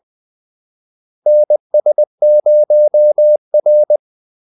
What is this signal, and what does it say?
Morse code, heard as a keyed tone, about 20 wpm, 600 Hz. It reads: NS0R
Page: Kaldesignaler 20 wpm